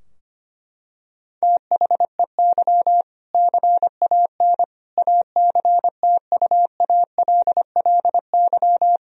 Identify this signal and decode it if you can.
Morse code, heard as a keyed tone, about 25 wpm, 700 Hz. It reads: THEY CAN ACTUALLY
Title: Sentences from Top 1000 Words - Review Part 4 - 25wpm